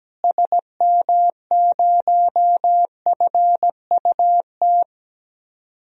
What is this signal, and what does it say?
Morse code, heard as a keyed tone, about 17 wpm, 700 Hz. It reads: SM0FUT
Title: DXCC-100 Call Signs - 6 Characters - Repeated 2x - 17wpm